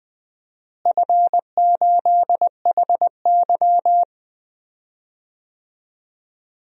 Morse code, heard as a keyed tone, about 20 wpm, 700 Hz. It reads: F8HY
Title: DXCC-100 Call Signs - 4 Characters - Repeated 3x - 20wpm